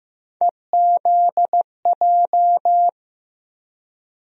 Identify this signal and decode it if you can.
Morse code, heard as a keyed tone, about 15 wpm, 700 Hz. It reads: EZJ
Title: USA Call Sign Postfixes - 3 Letters 15wpm